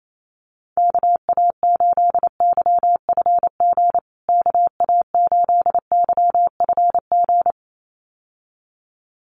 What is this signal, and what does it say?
Morse code, heard as a keyed tone, about 28 wpm, 700 Hz. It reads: KA8YFG KA8YFG
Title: USA Call Signs Repeated 2x 28wpm